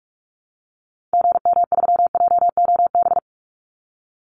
Morse code, heard as a keyed tone, about 44 wpm, 700 Hz. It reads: GM3JOB
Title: Call Signs Repeated 2x - 45wpm